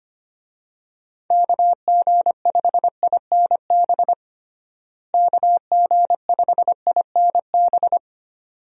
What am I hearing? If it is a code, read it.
Morse code, heard as a keyed tone, about 25 wpm, 700 Hz. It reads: KG5INB KG5INB
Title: USA Call Signs - 6 Characters - Repeated 2x - 25wpm